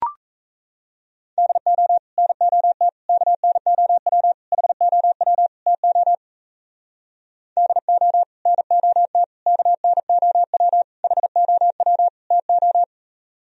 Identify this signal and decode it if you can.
Morse code, heard as a keyed tone, about 38 wpm, 700 Hz. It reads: DO NOT KNOW HOW TO DO NOT KNOW HOW TO